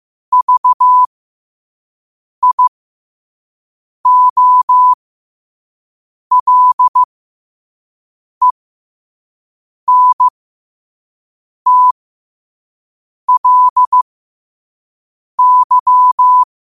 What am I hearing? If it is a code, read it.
Morse code, heard as a keyed tone, about 15 wpm, spaced as for 5 wpm, 1000 Hz. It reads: VIOLENTLY
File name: qotdfiles / 5wpm